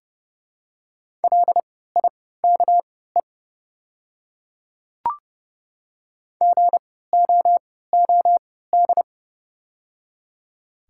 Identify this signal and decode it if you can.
Morse code, heard as a keyed tone, about 29 wpm, spaced as for 17 wpm, 700 Hz. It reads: LIKE GOOD
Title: 3x Character Spacing - Top 500 Words - 30wpm